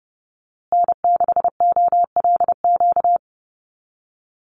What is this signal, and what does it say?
Morse code, heard as a keyed tone, about 30 wpm, 700 Hz. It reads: N6OLQ
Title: USA Call Signs Repeated 3x 30wpm